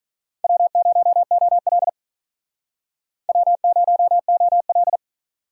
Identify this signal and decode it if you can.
Morse code, heard as a keyed tone, about 47 wpm, 700 Hz. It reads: W0OL W0OL